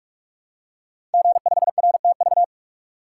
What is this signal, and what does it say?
Morse code, heard as a keyed tone, about 44 wpm, 700 Hz. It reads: G5RTV
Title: Code Groups - 5 Letter:Numbers - 45wpm